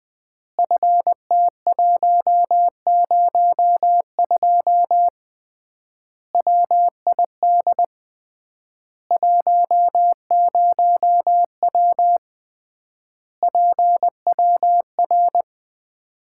Morse code, heard as a keyed tone, about 20 wpm, 700 Hz. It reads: FT102 WID 10W PWR
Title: Common QSO Phrases - 3x Word-Spacing - 20wpm